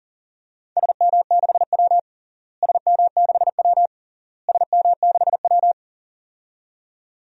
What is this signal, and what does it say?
Morse code, heard as a keyed tone, about 40 wpm, 700 Hz. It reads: SM6W SM6W SM6W
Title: DXCC-100 Call Signs - 4 Characters - Repeated 3x - 40wpm